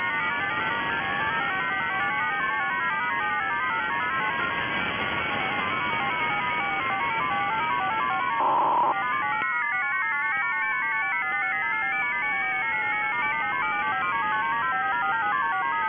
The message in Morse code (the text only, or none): none